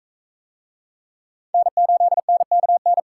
Morse code, heard as a keyed tone, about 39 wpm, 700 Hz. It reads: N8NKN